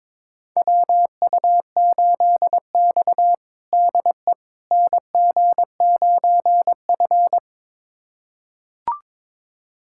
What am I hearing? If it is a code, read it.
Morse code, heard as a keyed tone, about 22 wpm, 700 Hz. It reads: WU8X DE NG9F
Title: USA Callsign DE Callsign - 4 Characters - 22wpm